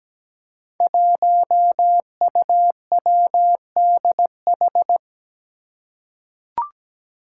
Morse code, heard as a keyed tone, about 17 wpm, 700 Hz. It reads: 1UWDH
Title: Five Challenging Letter-Number Combinations 17wpm